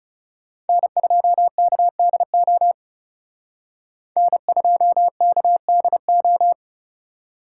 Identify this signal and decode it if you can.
Morse code, heard as a keyed tone, about 38 wpm, 700 Hz. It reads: N2KDO N2KDO